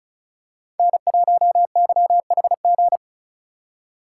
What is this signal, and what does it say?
Morse code, heard as a keyed tone, about 36 wpm, 700 Hz. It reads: N1YHG
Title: USA Call Signs - 3x Character Spacing - 35wpm